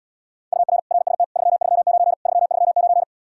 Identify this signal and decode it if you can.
Morse code, heard as a keyed tone, about 79 wpm, 700 Hz. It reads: UR RST 339 339